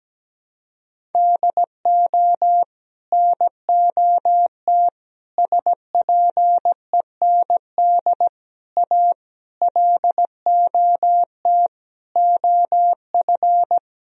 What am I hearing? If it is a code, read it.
Morse code, heard as a keyed tone, about 17 wpm, 700 Hz. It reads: DO NOT SPEND A LOT OF
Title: Sentences from Top 1000 Words - Review Part 4 - 17wpm